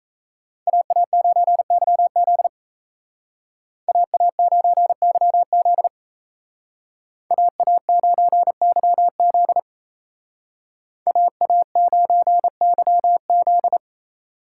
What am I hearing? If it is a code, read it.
Morse code, heard as a keyed tone, about 33 wpm, 700 Hz. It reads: AA9YZ AA9YZ AA9YZ AA9YZ
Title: Speed Racing - USA Call Signs 28wpm